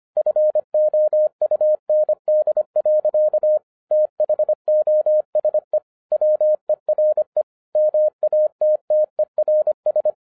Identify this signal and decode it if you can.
Morse code, heard as a keyed tone, about 25 wpm, 610 Hz. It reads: FOUND. THOSE WERE MATTERS
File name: 2011-07-mp3 / n59 BACK FROM A BIG ACCIDENT 25wpm